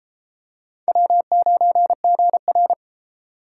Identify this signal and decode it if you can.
Morse code, heard as a keyed tone, about 32 wpm, 700 Hz. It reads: W9GR